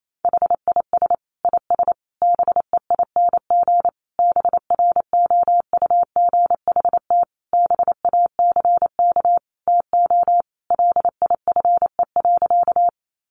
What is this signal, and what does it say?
Morse code, heard as a keyed tone, about 28 wpm, 700 Hz. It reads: HIS IS BEING BROUGHT BACK TO LIFE.
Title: Sentences from Top 600 Words 28wpm